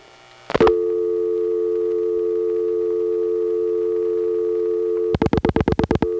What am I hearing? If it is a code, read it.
Dial tone (North America), 350 + 440 Hz.